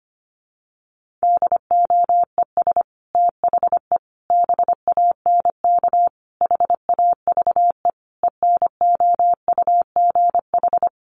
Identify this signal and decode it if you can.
Morse code, heard as a keyed tone, about 25 wpm, 700 Hz. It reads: DOES THE BANK HAVE ENOUGH
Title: Sentences from Top 1500 Words 25wpm